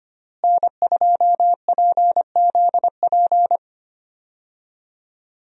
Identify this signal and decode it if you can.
Morse code, heard as a keyed tone, about 25 wpm, 700 Hz. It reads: N2PZP
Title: USA Call Signs - 5 Characters - Repeated 2x - 25wpm